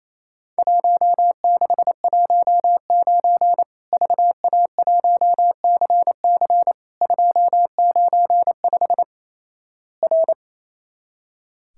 Morse code, heard as a keyed tone, about 28 wpm, 700 Hz. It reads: 1619 VA1CC 295 R
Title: Worked All Europe DX Contest - QTC Exchange - 28wpm